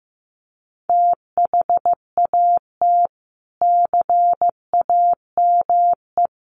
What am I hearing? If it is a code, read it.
Morse code, heard as a keyed tone, about 15 wpm, 700 Hz. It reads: THAT CAME
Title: Sentences from Top 1300 Words 15wpm